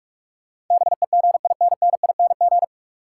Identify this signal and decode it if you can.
Morse code, heard as a keyed tone, about 42 wpm, 700 Hz. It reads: BEGINNING
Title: Sets of 3 Words - New Word Drill - 45wpm